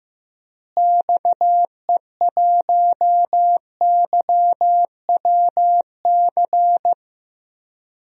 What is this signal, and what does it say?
Morse code, heard as a keyed tone, about 15 wpm, 700 Hz. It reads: XE1YWC
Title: DXCC-100 Call Signs - 6 Characters - Repeated 3x - 15wpm